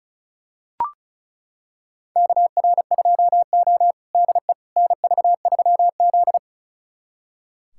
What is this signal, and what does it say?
Morse code, heard as a keyed tone, about 36 wpm, 700 Hz. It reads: KR2O DE NV3Z
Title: USA Callsign DE Callsign - 4 Characters - 35wpm